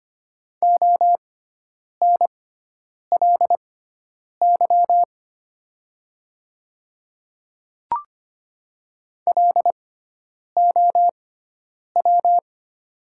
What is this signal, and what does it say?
Morse code, heard as a keyed tone, about 25 wpm, spaced as for 9 wpm, 690 Hz. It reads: ONLY LOW
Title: 6x Character Spacing - Top 200 Words - 25wpm